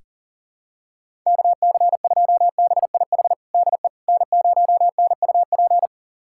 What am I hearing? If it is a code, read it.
Morse code, heard as a keyed tone, about 40 wpm, 700 Hz. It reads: KC2BIH DE N0NUP
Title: USA Callsign DE Callsign 40wpm